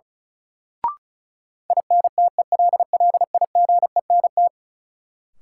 Morse code, heard as a keyed tone, about 34 wpm, 700 Hz. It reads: INTELLIGENT
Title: Sentences from Top 1000 Words - Review Part 7 - New Word Drill - 35wpm